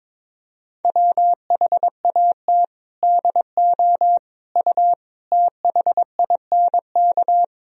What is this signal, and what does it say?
Morse code, heard as a keyed tone, about 22 wpm, 700 Hz. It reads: WHAT DO U THINK
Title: Sentences from Top 200 Words and Abbreviations 22wpm